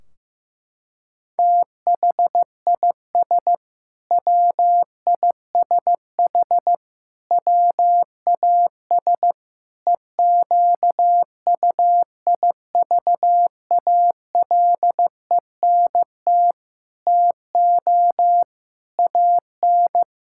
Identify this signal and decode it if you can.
Morse code, heard as a keyed tone, about 15 wpm, 700 Hz. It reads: THIS WISH WAS EQUIVALENT TO AN
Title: Sentences from Top 1600 Words 15wpm